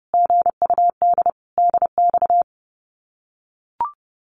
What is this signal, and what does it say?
Morse code, heard as a keyed tone, about 30 wpm, 700 Hz. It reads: GUD DX
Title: Common QSO Phrases Encore 30wpm